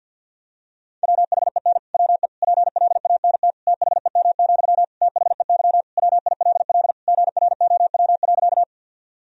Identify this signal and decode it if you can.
Morse code, heard as a keyed tone, about 53 wpm, 700 Hz. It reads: WHEN WE PLANT THEM, THEY WILL GROW.
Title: Sentences from Top 1000 Words - Review Part 1 - 50wpm